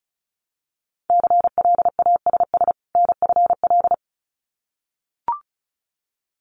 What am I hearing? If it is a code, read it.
Morse code, heard as a keyed tone, about 36 wpm, 700 Hz. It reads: CLASS NFL
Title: Winter Field Day Exchanges 35wpm